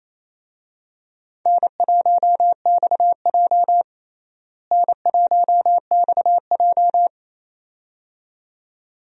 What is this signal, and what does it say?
Morse code, heard as a keyed tone, about 28 wpm, 700 Hz. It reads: N1XJ N1XJ
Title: USA Call Signs - 4 Characters - Repeated 2x - 28wpm